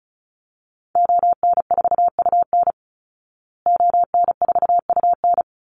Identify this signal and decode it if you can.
Morse code, heard as a keyed tone, about 34 wpm, 700 Hz. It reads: ON4UN ON4UN
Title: Call Signs Repeated 2x - 35wpm